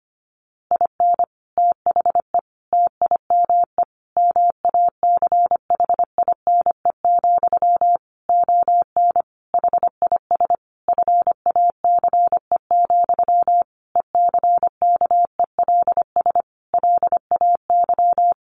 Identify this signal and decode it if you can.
Morse code, heard as a keyed tone, about 25 wpm, 700 Hz. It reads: IN THE TIME MACHINE, ON HIS FACE, ECKELS LAY
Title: A Sound Of Thunder 25wpm